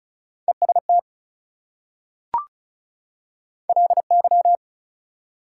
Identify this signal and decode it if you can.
Morse code, heard as a keyed tone, about 36 wpm, 700 Hz. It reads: EST LY
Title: Prefixes and Suffixes 35wpm